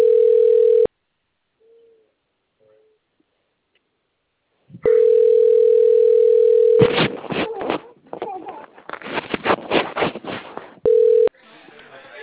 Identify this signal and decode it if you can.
Ringback tone (North America), 440 + 480 Hz, with speech.
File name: out- - - - - .25028